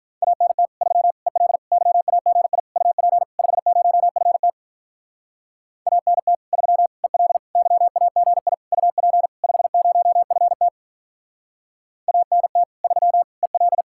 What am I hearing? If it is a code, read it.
Morse code, heard as a keyed tone, about 46 wpm, 700 Hz. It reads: ANT 3 EL YAGI UP 50FT ANT 3 EL YAGI UP 50FT ANT 3 EL